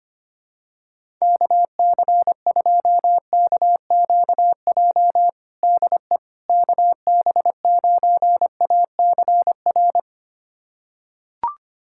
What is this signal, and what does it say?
Morse code, heard as a keyed tone, about 25 wpm, 700 Hz. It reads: KC2KQJ DE KB9ACR
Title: USA Callsign DE Callsign - 6 Characters - 25wpm